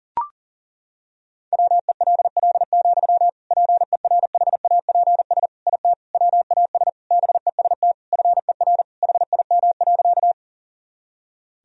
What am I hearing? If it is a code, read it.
Morse code, heard as a keyed tone, about 40 wpm, 700 Hz. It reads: WELL, PERHAPS IT WAS BEST FER HIM.
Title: Sentences from Top 500 Words and Abbreviations 40wpm